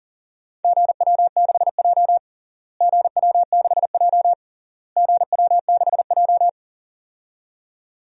Morse code, heard as a keyed tone, about 40 wpm, 700 Hz. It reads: GW6J GW6J GW6J
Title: DXCC-100 Call Signs - 4 Characters - Repeated 3x - 40wpm